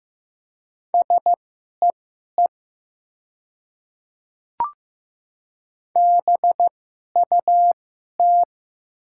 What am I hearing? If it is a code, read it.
Morse code, heard as a keyed tone, about 15 wpm, spaced as for 11 wpm, 700 Hz. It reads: SEE BUT
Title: 2x Character Spacing - Top 100 Words - 15wpm